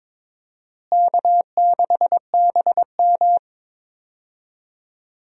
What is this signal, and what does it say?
Morse code, heard as a keyed tone, about 22 wpm, 700 Hz. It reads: K6BM
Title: USA Call Signs - 4 Characters - Repeated 2x - 22wpm